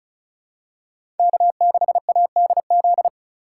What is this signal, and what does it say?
Morse code, heard as a keyed tone, about 36 wpm, 690 Hz. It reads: K6ADZ
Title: USA Call Signs - 5 Characters - 3x Character Spacing - 35wpm